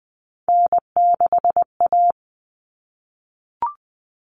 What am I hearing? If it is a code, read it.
Morse code, heard as a keyed tone, about 20 wpm, 700 Hz. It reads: N6A
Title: All Cut Numbers - 3-Digit 20wpm